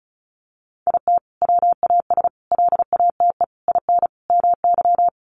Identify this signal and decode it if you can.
Morse code, heard as a keyed tone, about 34 wpm, 700 Hz. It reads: IT WAS LATE IN MY
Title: Sentences from Top 1300 Words 35wpm